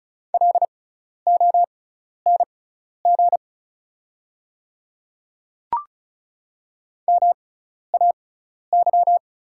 Morse code, heard as a keyed tone, about 36 wpm, spaced as for 12 wpm, 690 Hz. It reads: LONG MAY
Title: 6x Character Spacing - Top 100 Words - 35wpm